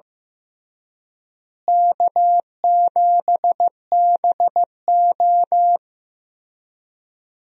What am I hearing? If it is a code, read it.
Morse code, heard as a keyed tone, about 15 wpm, 700 Hz. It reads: K7BO
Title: USA Call Signs - 4 Characters - Repeated 3x - 15wpm